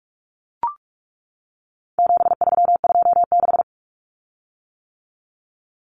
Morse code, heard as a keyed tone, about 44 wpm, 700 Hz. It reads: 7326
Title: Common Cut Numbers - 4-Digit 45wpm